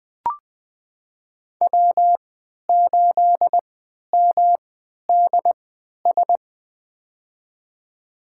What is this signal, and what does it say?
Morse code, heard as a keyed tone, about 21 wpm, spaced as for 12 wpm, 700 Hz. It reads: W8MDS
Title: USA Call Signs - 5 Characters - 3x Character Spacing - 20wpm